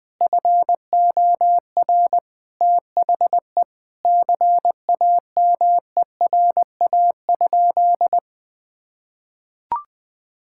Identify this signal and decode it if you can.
Morse code, heard as a keyed tone, about 20 wpm, 700 Hz. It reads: FOR THE CAMERA?
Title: Sentences from Top 1000 Words - Review Part 5 - 20wpm